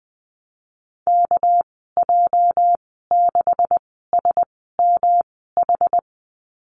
Morse code, heard as a keyed tone, about 20 wpm, spaced as for 15 wpm, 700 Hz. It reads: KJ6SMH
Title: 245 Lesson 39g - Letter z - Call Signs 20wpm